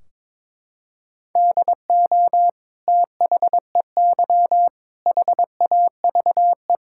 Morse code, heard as a keyed tone, about 21 wpm, 700 Hz. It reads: DO THEY HAVE
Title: Sentences from Top 1600 Words 22wpm